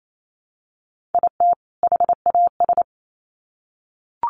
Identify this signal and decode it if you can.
Morse code, heard as a keyed tone, about 28 wpm, 700 Hz. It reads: IT HAS
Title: Sets of 2 Words 28wpm